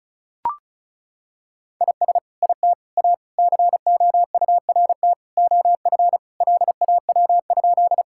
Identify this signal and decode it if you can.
Morse code, heard as a keyed tone, about 36 wpm, 700 Hz. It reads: IS IT A COURT OF LAW?
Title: Sentences from Top 1600 Words 35wpm